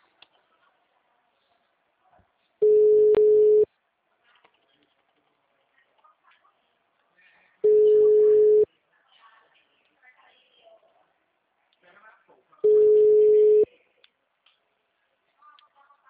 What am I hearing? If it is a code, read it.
Ringback tone (Europe, 425 Hz), with speech.